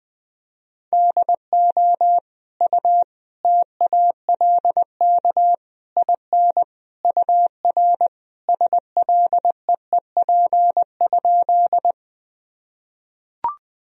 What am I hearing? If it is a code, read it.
Morse code, heard as a keyed tone, about 21 wpm, 700 Hz. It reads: DO U TALK IN UR SLEEP?
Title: Sentences from Top 500 Words and Abbreviations 20wpm